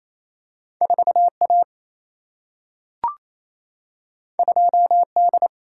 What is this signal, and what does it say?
Morse code, heard as a keyed tone, about 28 wpm, 700 Hz. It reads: 4A 2D
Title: Single-Letter Single-Number 28wpm